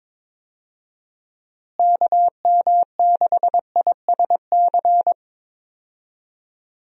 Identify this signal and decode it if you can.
Morse code, heard as a keyed tone, about 22 wpm, 700 Hz. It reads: KM6ISC
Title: USA Call Signs - 6 Characters - Repeated 3x - 22wpm